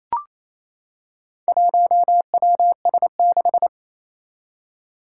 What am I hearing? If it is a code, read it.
Morse code, heard as a keyed tone, about 27 wpm, 700 Hz. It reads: 1WS6
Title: Four Challenging Letter-Number Combinations 28wpm